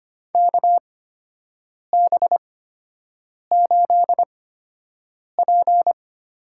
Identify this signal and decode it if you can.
Morse code, heard as a keyed tone, about 25 wpm, spaced as for 7 wpm, 700 Hz. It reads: KB8P
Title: USA Call Signs - 6 Characters - 8x Character Spacing - 25wpm